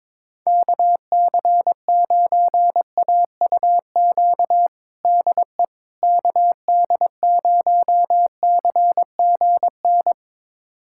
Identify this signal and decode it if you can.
Morse code, heard as a keyed tone, about 22 wpm, 700 Hz. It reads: KC9AUQ DE KD0CGN
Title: USA Callsign DE Callsign - 6 Characters - 22wpm